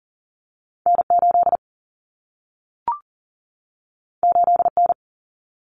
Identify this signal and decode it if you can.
Morse code, heard as a keyed tone, about 40 wpm, 700 Hz. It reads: N8 8N
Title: Common Cut Numbers - 2-Digit 40wpm